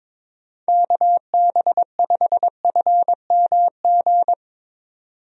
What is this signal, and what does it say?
Morse code, heard as a keyed tone, about 22 wpm, 700 Hz. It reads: KB5FMG
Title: USA Call Signs - 6 Characters - 22wpm